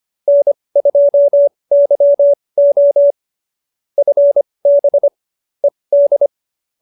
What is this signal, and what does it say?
Morse code, heard as a keyed tone, about 25 wpm, spaced as for 20 wpm, 590 Hz. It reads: N2YO FB ED
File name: QSO316 20